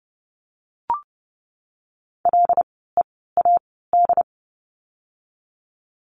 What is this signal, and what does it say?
Morse code, heard as a keyed tone, about 30 wpm, spaced as for 17 wpm, 700 Hz. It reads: LEAD
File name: Lesson 16f - Letter c - Words 30wpm